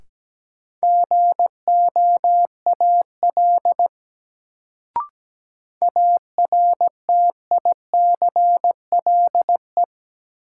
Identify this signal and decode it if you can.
Morse code, heard as a keyed tone, about 17 wpm, 700 Hz. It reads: GOAL ARTICLE
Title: Sentences from Top 1000 Words - Review Part 5 - New Word Drill - 17wpm